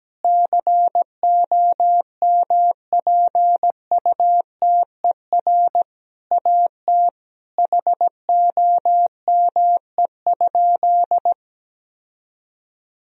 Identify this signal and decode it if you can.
Morse code, heard as a keyed tone, about 17 wpm, 700 Hz. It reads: COMPUTER AT HOME?
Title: Sentences from Top 1000 Words - Review Part 3 - 17wpm